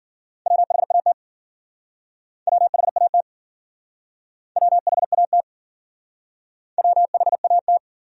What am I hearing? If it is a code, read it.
Morse code, heard as a keyed tone, about 53 wpm, 700 Hz. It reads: WHAT WHAT WHAT WHAT